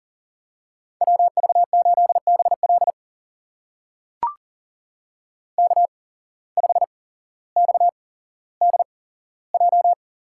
Morse code, heard as a keyed tone, about 40 wpm, spaced as for 17 wpm, 700 Hz. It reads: WV8BL K5XDJ
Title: USA Call Signs - 8x Character Spacing - 40wpm